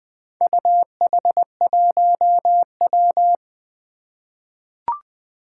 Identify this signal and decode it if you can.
Morse code, heard as a keyed tone, about 20 wpm, 700 Hz. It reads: UH1W
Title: Four Challenging Letter-Number Combinations 20wpm